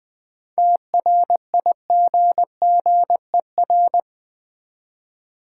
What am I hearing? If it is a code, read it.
Morse code, heard as a keyed tone, about 20 wpm, 700 Hz. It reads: TRIGGER
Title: 7 Letter Words - First 3 Letters Given - 20wpm